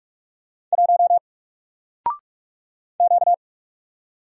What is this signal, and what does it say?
Morse code, heard as a keyed tone, about 44 wpm, 700 Hz. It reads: 1 Q
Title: Single Character (Letter:Number:Punctuation:Prosign) 45wpm